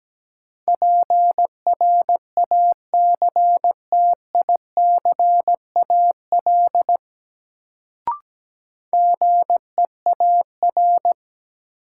Morse code, heard as a keyed tone, about 17 wpm, 700 Hz. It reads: PRACTICAL GEAR
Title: Sentences from Top 1400 Words - New Word Drill - 17wpm